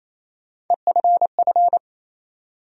Morse code, heard as a keyed tone, about 28 wpm, 700 Hz. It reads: EFF
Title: USA Call Sign Postfixes - 3 Letters 28wpm